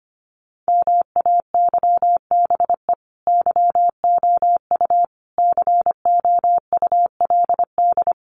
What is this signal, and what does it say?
Morse code, heard as a keyed tone, about 25 wpm, 700 Hz. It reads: MAYBE YOU COULD
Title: Sentences from Top 1400 Words 25wpm